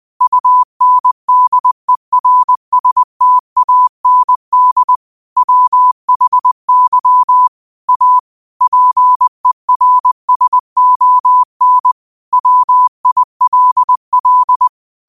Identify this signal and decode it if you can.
Morse code, heard as a keyed tone, about 20 wpm, 1000 Hz. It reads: UNDERSTAND WHY A PERSON WILL